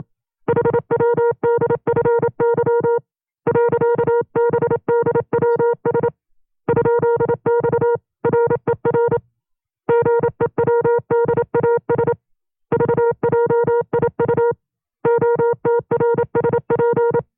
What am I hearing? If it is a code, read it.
Morse code, heard as a keyed tone, about 27 wpm, 470 Hz. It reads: HWDFY .BDWS ?X RER GEWDAS VJIU OTRSP
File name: FreeFiles / CW-practice / 25WPM-side1